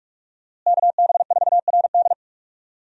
Morse code, heard as a keyed tone, about 46 wpm, 700 Hz. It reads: KB4RD
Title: USA Call Signs - 5 Characters - Repeated 2x - 45wpm